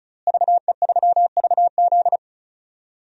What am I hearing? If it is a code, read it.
Morse code, heard as a keyed tone, about 34 wpm, 700 Hz. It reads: VE3VZ
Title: DXCC-100 Call Signs - 5 Characters - Repeated 2x - 35wpm